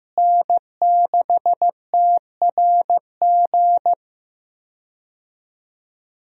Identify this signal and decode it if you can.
Morse code, heard as a keyed tone, about 16 wpm, 700 Hz. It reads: N6TRG
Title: USA Call Signs - 5 Characters - Repeated 3x - 15wpm